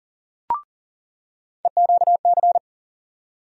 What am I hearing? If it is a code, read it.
Morse code, heard as a keyed tone, about 40 wpm, 700 Hz. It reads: EQC
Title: USA Call Sign Postfixes - 3 Letters 40wpm